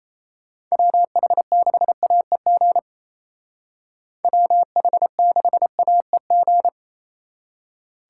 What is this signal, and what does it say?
Morse code, heard as a keyed tone, about 31 wpm, 700 Hz. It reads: WH6AEG WH6AEG
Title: Speed Racing - USA 2x3 Call Signs 28wpm